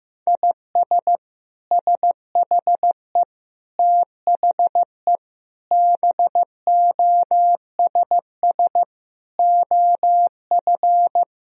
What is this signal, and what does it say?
Morse code, heard as a keyed tone, about 15 wpm, 700 Hz. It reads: IS SHE THE BOSS OF
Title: Sentences from Top 1000 Words - Review Part 4 - 15wpm